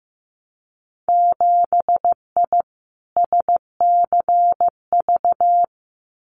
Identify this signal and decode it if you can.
Morse code, heard as a keyed tone, about 15 wpm, 700 Hz. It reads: 7I SCV
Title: Winter Field Day Exchanges Encore 15wpm